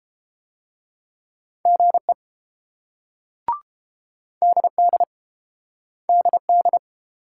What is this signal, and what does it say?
Morse code, heard as a keyed tone, about 28 wpm, 700 Hz. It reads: GE DD DD